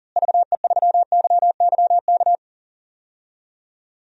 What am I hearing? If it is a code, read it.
Morse code, heard as a keyed tone, about 40 wpm, 700 Hz. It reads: VE3YYK
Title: DXCC-100 Call Signs - 6 Characters - Repeated 3x - 40wpm